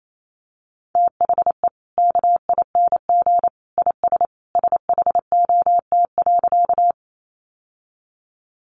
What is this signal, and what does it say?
Morse code, heard as a keyed tone, about 28 wpm, 700 Hz. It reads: THE KING IS SHOT.
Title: Sentences from Top 1100 Words 28wpm